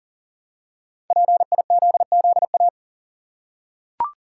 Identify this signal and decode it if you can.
Morse code, heard as a keyed tone, about 39 wpm, 700 Hz. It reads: PIZZA
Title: Five Letter Words - 40wpm